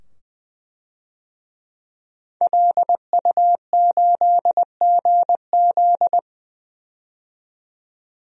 Morse code, heard as a keyed tone, about 20 wpm, 700 Hz. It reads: LU8GZ
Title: DXCC-100 Call Signs - 5 Characters - 20wpm